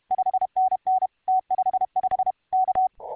Morse code, heard as a keyed tone, about 32 wpm, 750 Hz. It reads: 5NN T55 K